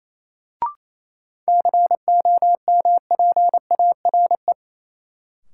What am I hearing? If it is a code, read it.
Morse code, heard as a keyed tone, about 28 wpm, 700 Hz. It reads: COMPARE
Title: Sentences from Top 1300 Words - New Word Drill Alt - 28wpm